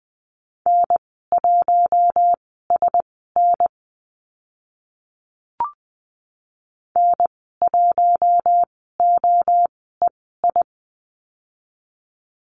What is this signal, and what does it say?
Morse code, heard as a keyed tone, about 20 wpm, spaced as for 14 wpm, 700 Hz. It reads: N1SN N1OEI
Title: 042 Lesson 8g - Number 1 - Call Signs 20wpm